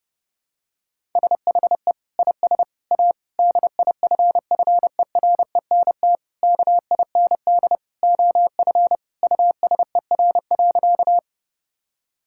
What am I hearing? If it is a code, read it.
Morse code, heard as a keyed tone, about 30 wpm, 700 Hz. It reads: SHE IS A DIFFERENT KIND OF USER.
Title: Sentences from Top 1000 Words - Review Part 5 - 30wpm